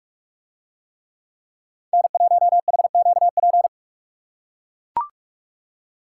Morse code, heard as a keyed tone, about 44 wpm, 700 Hz. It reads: N1HQP